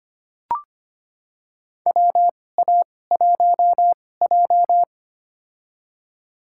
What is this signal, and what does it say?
Morse code, heard as a keyed tone, about 25 wpm, spaced as for 18 wpm, 700 Hz. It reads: WA1J
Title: USA Call Signs - 4 Characters - 2x Character Spacing - 25wpm